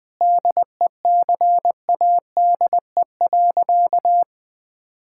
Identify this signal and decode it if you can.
Morse code, heard as a keyed tone, about 20 wpm, 700 Hz. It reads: DECADE.
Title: Sentences with 7-Word Phrases - Spoken Beginning - 20wpm